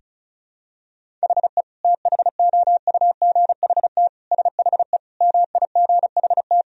Morse code, heard as a keyed tone, about 36 wpm, 700 Hz. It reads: HE THOUGHT SHE MIGHT
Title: Sentences from Top 1000 Words - Review Part 1 - 35wpm